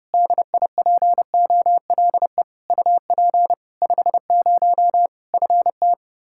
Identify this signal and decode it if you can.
Morse code, heard as a keyed tone, about 32 wpm, 700 Hz. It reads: DIPOLE UP 50 FT
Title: Practice QSO - Exchange 2 - Info - 30wpm